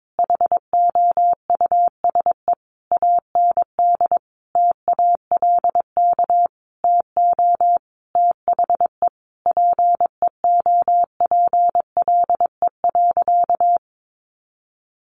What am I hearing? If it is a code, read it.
Morse code, heard as a keyed tone, about 22 wpm, 700 Hz. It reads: HOUSE AND TALK TO THE PEOPLE.
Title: Sentences from Top 400 Words 22wpm